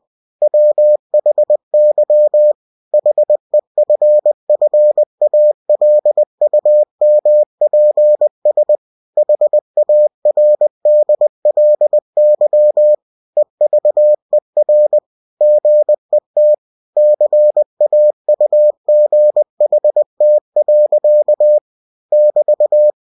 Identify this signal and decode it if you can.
Morse code, heard as a keyed tone, about 20 wpm, 600 Hz. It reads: WHY HEFFALUMPS HARDLY EVER GET CAUGHT. <BT>
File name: cw-books / winnie-the-pooh / winnie-20wpm-page-0036